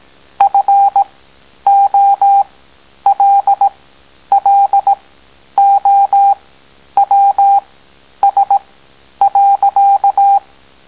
Morse code, heard as a keyed tone, about 18 wpm, spaced as for 10 wpm, 800 Hz. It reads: FOLLOWS.